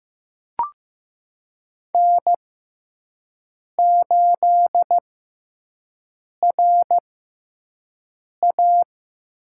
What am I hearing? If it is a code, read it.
Morse code, heard as a keyed tone, about 15 wpm, spaced as for 5 wpm, 700 Hz. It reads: N8RA
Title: USA Call Signs - 6x Character Spacing - 15wpm